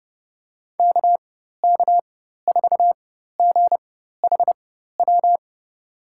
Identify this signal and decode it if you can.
Morse code, heard as a keyed tone, about 30 wpm, spaced as for 14 wpm, 700 Hz. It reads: KK4GHW
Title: USA Call Signs - 6 Characters - 4x Character Spacing - 30wpm